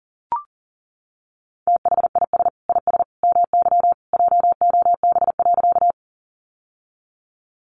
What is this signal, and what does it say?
Morse code, heard as a keyed tone, about 40 wpm, 700 Hz. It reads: THIS IS MY JOB.
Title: Sentences from Top 600 Words 40wpm